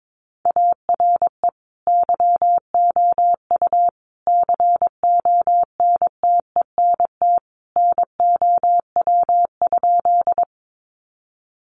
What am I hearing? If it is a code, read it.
Morse code, heard as a keyed tone, about 22 wpm, 700 Hz. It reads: ARE YOU CONTENT NOW?
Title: Sentences from Top 900 Words 22wpm